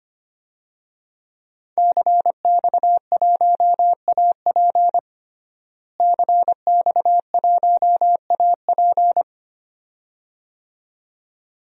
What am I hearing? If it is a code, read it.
Morse code, heard as a keyed tone, about 25 wpm, 700 Hz. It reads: CX1AP CX1AP
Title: DXCC-100 Call Signs - 5 Characters - Repeated 2x - 25wpm